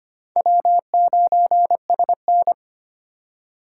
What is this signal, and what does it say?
Morse code, heard as a keyed tone, about 25 wpm, 700 Hz. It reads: W9SN
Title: USA Call Signs - 4 Characters - 6x Character Spacing - 25wpm